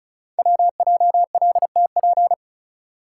Morse code, heard as a keyed tone, about 36 wpm, 700 Hz. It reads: WJLTP
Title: Code Groups - 5 Letters - 35wpm